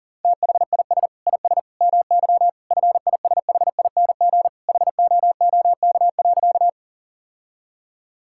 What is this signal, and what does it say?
Morse code, heard as a keyed tone, about 37 wpm, 700 Hz. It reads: THIS IS MY FISHING HOOK.
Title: Sentences from Top 1600 Words 40wpm